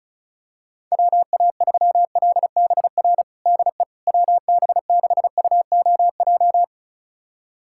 Morse code, heard as a keyed tone, about 34 wpm, 700 Hz. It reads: WA3LBR DE WB6UOJ
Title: USA Callsign DE Callsign - 6 Characters - 35wpm